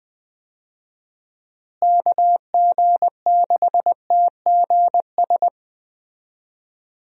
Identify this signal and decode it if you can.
Morse code, heard as a keyed tone, about 20 wpm, 700 Hz. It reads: KG6TGS
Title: USA Call Signs - 6 Characters - Repeated 2x - 20wpm